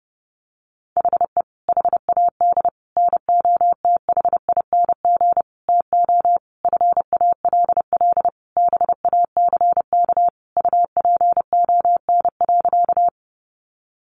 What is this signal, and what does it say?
Morse code, heard as a keyed tone, about 30 wpm, 700 Hz. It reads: HE HAD NOTHING TO FALL BACK UPON.
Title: Sentences from Top 1100 Words 30wpm